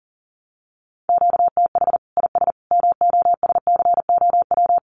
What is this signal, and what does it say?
Morse code, heard as a keyed tone, about 40 wpm, 700 Hz. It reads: QTH IS MOSCOW
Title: Common QSO Phrases 40wpm